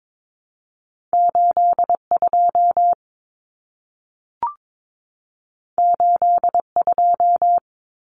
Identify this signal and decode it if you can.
Morse code, heard as a keyed tone, about 22 wpm, 700 Hz. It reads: 82 82
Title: Common Cut Numbers - 2-Digit 22wpm